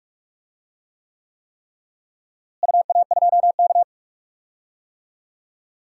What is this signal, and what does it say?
Morse code, heard as a keyed tone, about 47 wpm, 700 Hz. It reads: UA2K
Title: DXCC-100 Call Signs - 4 Characters - Repeated 3x - 45wpm